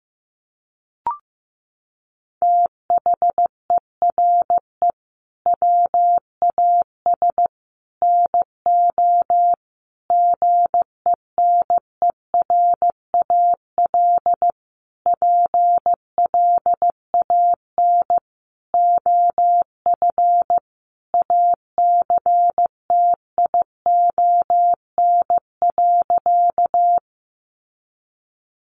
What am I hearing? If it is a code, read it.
Morse code, heard as a keyed tone, about 15 wpm, 700 Hz. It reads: THERE WAS NO GENERAL PLAN OF ACTION.
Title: Sentences from Top 700 Words 15wpm